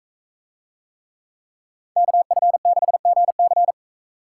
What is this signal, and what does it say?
Morse code, heard as a keyed tone, about 42 wpm, 700 Hz. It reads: KF6GC